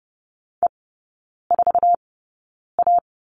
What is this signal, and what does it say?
Morse code, heard as a keyed tone, about 30 wpm, spaced as for 9 wpm, 700 Hz. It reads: E4A
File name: Lesson 17e - Letter u - Three Characters 30wpm